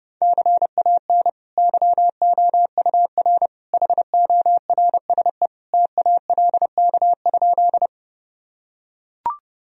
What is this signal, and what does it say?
Morse code, heard as a keyed tone, about 32 wpm, 700 Hz. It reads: CAN YOUR HORSE TALK?
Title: Sentences from Top 1000 Words - Review Part 1 - 30wpm